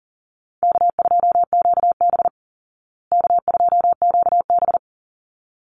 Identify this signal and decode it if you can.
Morse code, heard as a keyed tone, about 40 wpm, 700 Hz. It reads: K2QB K2QB
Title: Call Signs Repeated 3x - 40wpm